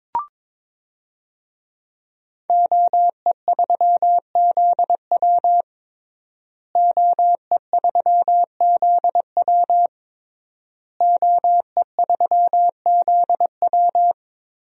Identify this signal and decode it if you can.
Morse code, heard as a keyed tone, about 22 wpm, 700 Hz. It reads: OE3ZW OE3ZW OE3ZW
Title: DXCC-100 Call Signs - 5 Characters - Repeated 3x - 22wpm